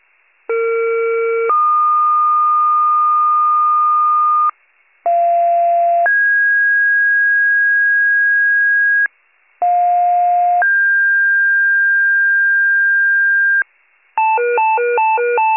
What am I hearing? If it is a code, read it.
Two-tone page, heard as a one-second tone, then a three-second tone, about 460 then 1220 Hz, about 690 then 1650 Hz, about 710 then 1600 Hz. It is heard from 0.5 to 13.6 s.